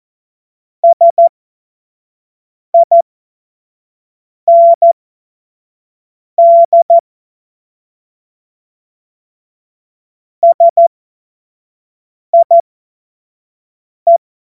Morse code, heard as a keyed tone, about 14 wpm, spaced as for 5 wpm, 680 Hz. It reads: SIND SIE
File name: Morsekurs / Witze / Witz 13. Januar 2026 25 F70